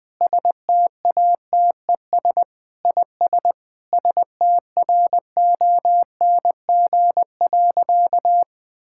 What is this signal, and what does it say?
Morse code, heard as a keyed tone, about 20 wpm, 700 Hz. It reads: STATES IS STRONG.
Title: Sentences with 8-Word Phrases - 20wpm